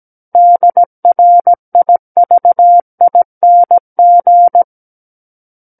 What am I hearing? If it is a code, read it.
Morse code, heard as a keyed tone, about 17 wpm, 700 Hz. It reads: DRIVING
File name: Book One / 08-Chapter 8 - 17 WPM